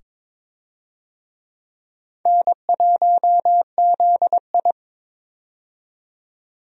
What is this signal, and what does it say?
Morse code, heard as a keyed tone, about 21 wpm, 700 Hz. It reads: N1ZI
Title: USA Call Signs - 4 Characters - 22wpm